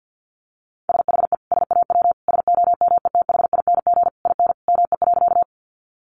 Morse code, heard as a keyed tone, about 50 wpm, 700 Hz. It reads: SHE SAW SOMETHING IN ME.
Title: Sentences from Top 600 Words 50wpm